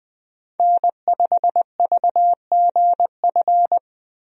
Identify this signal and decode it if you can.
Morse code, heard as a keyed tone, about 20 wpm, 700 Hz. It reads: N5VGF
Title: USA Call Signs - 5 Characters - Repeated 3x - 20wpm